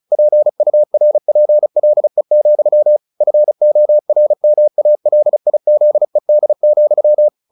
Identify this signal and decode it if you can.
Morse code, heard as a keyed tone, about 34 wpm, 600 Hz. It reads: PURPLE, FORMALIZED,
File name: the old man and the sea / the old man and the sea 1 35wpm